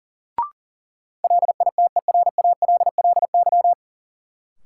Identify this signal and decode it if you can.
Morse code, heard as a keyed tone, about 40 wpm, 700 Hz. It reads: LITERALLY
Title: Sentences from Top 1400 Words - New Word Drill - 40wpm